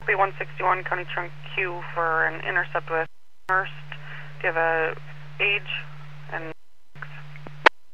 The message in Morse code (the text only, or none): none